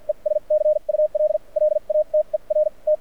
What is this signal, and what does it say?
Morse code, heard as a keyed tone, about 52 wpm, spaced as for 40 wpm, 590 Hz. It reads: ES CUL LATER T
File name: archived / CRSnet